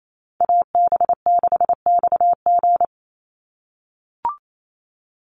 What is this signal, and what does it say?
Morse code, heard as a keyed tone, about 28 wpm, 700 Hz. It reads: AB6XG
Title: Call Signs - 28wpm